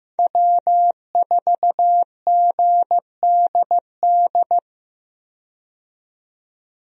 Morse code, heard as a keyed tone, about 15 wpm, 700 Hz. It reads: W4GDD
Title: USA Call Signs - 5 Characters - 15wpm